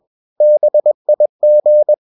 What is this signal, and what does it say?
Morse code, heard as a keyed tone, about 21 wpm, 600 Hz. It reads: BIG
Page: CW med Gnister